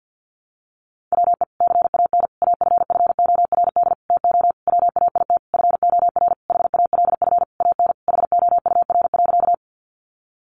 Morse code, heard as a keyed tone, about 50 wpm, 700 Hz. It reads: WE CAN AFFORD TO WAIT FOR HALF AN HOUR.
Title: Sentences from Top 1100 Words 50wpm